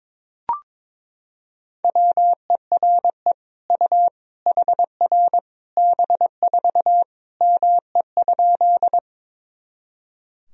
Morse code, heard as a keyed tone, about 22 wpm, 700 Hz. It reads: WERE U HR B4 ME?
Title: Sentences from Top 200 Words and Abbreviations 22wpm